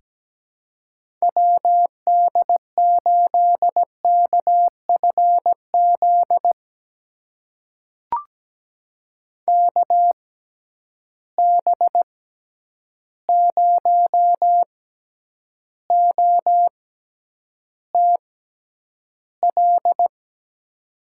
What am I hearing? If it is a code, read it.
Morse code, heard as a keyed tone, about 17 wpm, spaced as for 9 wpm, 700 Hz. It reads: WD8KFZ KB0OTL
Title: USA Call Signs - 6 Characters - 6x Character Spacing - 17wpm